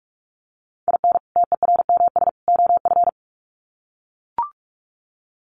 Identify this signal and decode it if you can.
Morse code, heard as a keyed tone, about 44 wpm, 700 Hz. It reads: IN TERMS OF
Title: Sets of 3 Words 45wpm